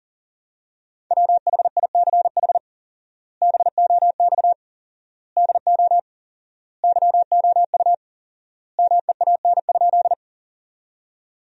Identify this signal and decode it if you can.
Morse code, heard as a keyed tone, about 40 wpm, 700 Hz. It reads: WHICH BOX DO YOU MEAN?
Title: 4x Word-Spacing - Sentences from Top 500 Words - 40wpm